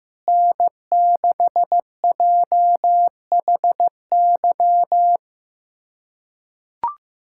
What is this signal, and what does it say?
Morse code, heard as a keyed tone, about 15 wpm, 700 Hz. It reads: N6JHY
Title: USA Callsign DE Callsign - 5 Characters - 15wpm